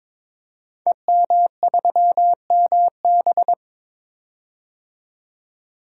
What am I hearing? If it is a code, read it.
Morse code, heard as a keyed tone, about 22 wpm, 690 Hz. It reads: EM3MB
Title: Code Groups - 5 Letter:Numbers - 22wpm